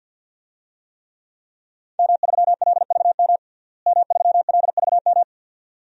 Morse code, heard as a keyed tone, about 50 wpm, 700 Hz. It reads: M3LVM M3LVM
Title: DXCC-100 Call Signs - 5 Characters - Repeated 2x - 50wpm